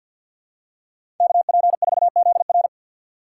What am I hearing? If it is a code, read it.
Morse code, heard as a keyed tone, about 49 wpm, 700 Hz. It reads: KP4ZR
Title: USA Call Signs - 5 Characters - 4x Character Spacing - 50wpm